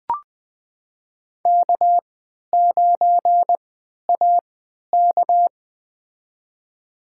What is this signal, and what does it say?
Morse code, heard as a keyed tone, about 20 wpm, spaced as for 11 wpm, 700 Hz. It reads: K9AK
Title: USA Call Signs - 4 Characters - 3x Character Spacing - 20wpm